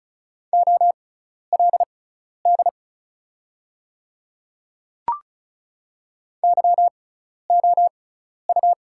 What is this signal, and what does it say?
Morse code, heard as a keyed tone, about 34 wpm, spaced as for 12 wpm, 700 Hz. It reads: OLD YOU
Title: 6x Character Spacing - Top 200 Words - 35wpm